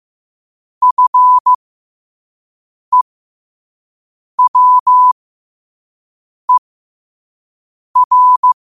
Morse code, heard as a keyed tone, about 15 wpm, spaced as for 5 wpm, 1000 Hz. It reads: FEWER
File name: qotdfiles / 5wpm